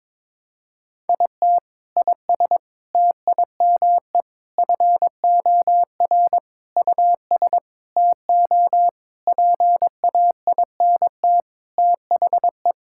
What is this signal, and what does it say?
Morse code, heard as a keyed tone, about 22 wpm, 700 Hz. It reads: IT IS TIME FOR US TO PAINT THE
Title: Sentences from Top 1000 Words - Review Part 6 - 22wpm